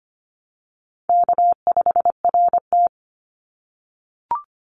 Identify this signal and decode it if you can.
Morse code, heard as a keyed tone, about 25 wpm, 700 Hz. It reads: K5RT
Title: Call Signs - 25wpm